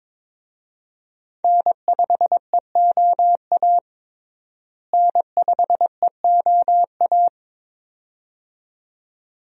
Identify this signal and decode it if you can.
Morse code, heard as a keyed tone, about 22 wpm, 700 Hz. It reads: N5EOA N5EOA
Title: USA Call Signs - 5 Characters - Repeated 2x - 22wpm